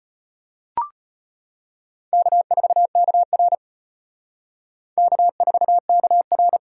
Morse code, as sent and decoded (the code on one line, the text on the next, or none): -.- ....- -.- .-. / -.- ....- -.- .-.
K4KR K4KR